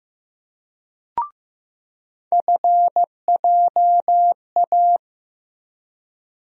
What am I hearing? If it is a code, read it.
Morse code, heard as a keyed tone, about 15 wpm, 700 Hz. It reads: FJA
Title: USA Call Sign Postfixes - 3 Letters 15wpm